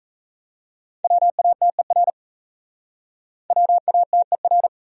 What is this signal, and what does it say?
Morse code, heard as a keyed tone, about 37 wpm, 700 Hz. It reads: WATER WATER